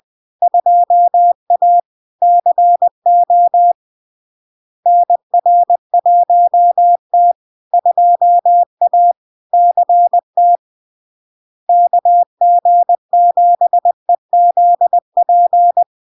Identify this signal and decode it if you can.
Morse code, heard as a keyed tone, about 20 wpm, 700 Hz. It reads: 2A CO NR1T 2A CT KG7EZP
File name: LARK-FD-2022-Log-in-CW-20wpm-700-Hz